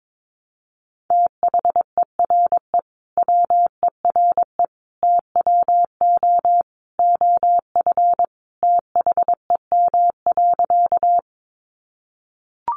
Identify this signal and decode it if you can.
Morse code, heard as a keyed tone, about 22 wpm, 700 Hz. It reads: THERE WERE TWO OF THEM.
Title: Sentences from Top 100 Words 22wpm